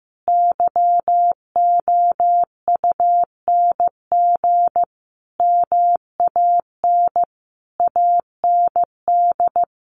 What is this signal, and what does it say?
Morse code, heard as a keyed tone, about 15 wpm, 700 Hz. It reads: YOUNG MAN AND
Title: Sentences from Top 1200 Words 15wpm